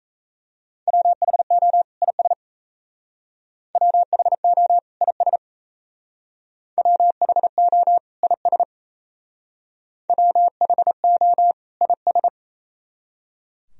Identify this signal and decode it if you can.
Morse code, heard as a keyed tone, about 32 wpm, 690 Hz. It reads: WHO IS WHO IS WHO IS WHO IS
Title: Speed-Racing - Sets of 2 Words 28wpm